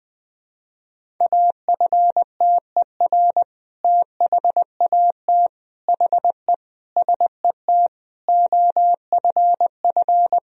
Morse code, heard as a keyed tone, about 20 wpm, 700 Hz. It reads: AFTER THAT HE SET OFF
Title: Sentences from Top 1000 Words - Review Part 1 - 20wpm